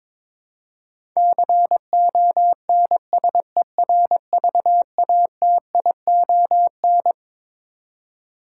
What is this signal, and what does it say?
Morse code, heard as a keyed tone, about 21 wpm, 700 Hz. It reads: CONSERVATION
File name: Twelve Letter Words - 22wpm